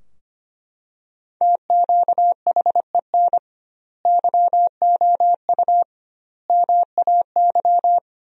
Morse code, heard as a keyed tone, about 25 wpm, 700 Hz. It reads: TQHEN YOU MAY
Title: 2x Word-Spacing - Sentences from Top 100 Words - 25wpm